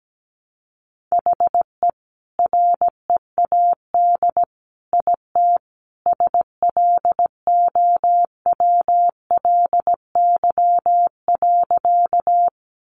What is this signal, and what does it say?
Morse code, heard as a keyed tone, about 17 wpm, 700 Hz. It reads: HE READ IT SLOWLY.
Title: Sentences from Top 1100 Words 17wpm